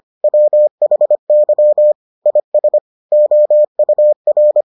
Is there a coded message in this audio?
Yes: Morse code from 0.2 to 4.6 s, heard as a keyed tone, about 25 wpm, 600 Hz. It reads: WHY IS OUR